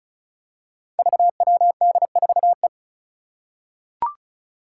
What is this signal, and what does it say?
Morse code, heard as a keyed tone, about 36 wpm, 700 Hz. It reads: VWD4E
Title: Five Challenging Letter-Number Combinations 35wpm